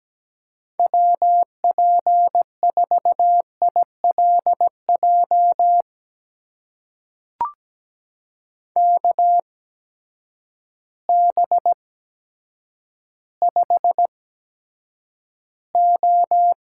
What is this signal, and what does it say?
Morse code, heard as a keyed tone, about 17 wpm, spaced as for 8 wpm, 700 Hz. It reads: WP4ILJ KB5O
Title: USA Call Signs - 8x Character Spacing - 17wpm